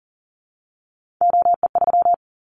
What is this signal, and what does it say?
Morse code, heard as a keyed tone, about 40 wpm, 700 Hz. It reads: OE3
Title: Callsign Prefixes for Top 100 DXCC - 40wpm